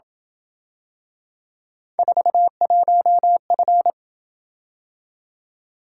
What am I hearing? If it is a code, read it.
Morse code, heard as a keyed tone, about 27 wpm, 700 Hz. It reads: 41F